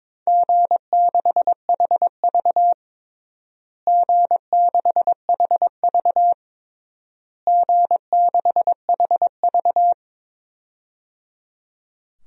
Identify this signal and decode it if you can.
Morse code, heard as a keyed tone, about 22 wpm, 700 Hz. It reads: G6HV G6HV G6HV
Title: DXCC-100 Call Signs - 4 Characters - Repeated 3x - 22wpm